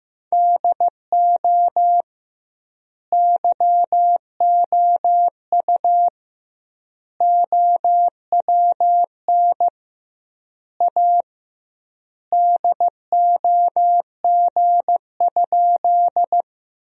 Morse code, heard as a keyed tone, about 15 wpm, 700 Hz. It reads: DO YOU OWN A DOG?
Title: 2x Word-Spacing - Sentences from Top 400 Words - 15wpm